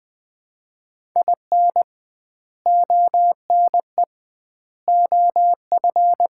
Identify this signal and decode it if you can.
Morse code, heard as a keyed tone, about 20 wpm, 700 Hz. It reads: IN ONE OF
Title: 2x Word-Spacing - Sets of 3 Words - 20wpm